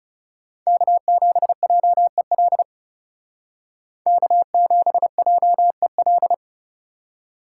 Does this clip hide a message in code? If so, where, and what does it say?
Yes: Morse code from 0.7 to 6.3 s, heard as a keyed tone, about 38 wpm, 690 Hz. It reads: K7JEL K7JEL